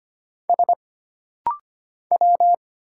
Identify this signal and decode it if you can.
Morse code, heard as a keyed tone, about 25 wpm, 700 Hz. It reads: S W
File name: Single Character (Letter:Number:Punctuation:Prosign) - Rapid-Fire - 25wpm